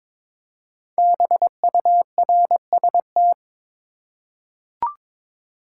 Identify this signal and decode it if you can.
Morse code, heard as a keyed tone, about 22 wpm, 700 Hz. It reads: BURST
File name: practice / 22wpm / 5 Letter Words - First 3 Letters Given - 22wpm